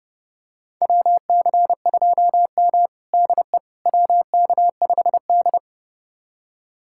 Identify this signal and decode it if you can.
Morse code, heard as a keyed tone, about 30 wpm, 700 Hz. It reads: WC2M DE WK5D
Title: USA Callsign DE Callsign - 4 Characters - 30wpm